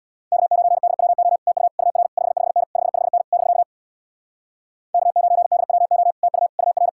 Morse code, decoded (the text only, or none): K9DWO TU UR 55N 55N <BK> K9DWO TU UR